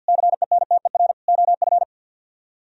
Morse code, heard as a keyed tone, about 63 wpm, 700 Hz. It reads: CENTER OF